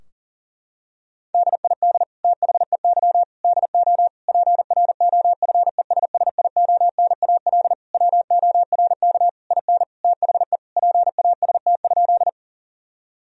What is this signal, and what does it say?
Morse code, heard as a keyed tone, about 40 wpm, 700 Hz. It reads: DID THEY DO PROFESSIONAL WORK IN THE PAST?
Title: Sentences from Top 1000 Words - Review Part 4 - 40wpm